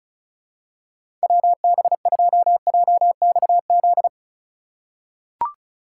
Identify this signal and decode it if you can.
Morse code, heard as a keyed tone, about 36 wpm, 700 Hz. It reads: WB2JXZ
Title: USA Call Signs - 6 Characters - 4x Character Spacing - 35wpm